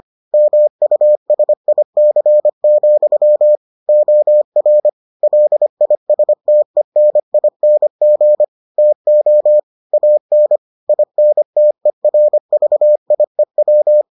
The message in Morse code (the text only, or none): MUSIC, OR LISTENING TO AN INTERVIEW